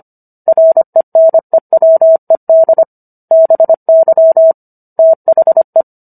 Morse code, heard as a keyed tone, about 25 wpm, 650 Hz. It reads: RENEWED BY THE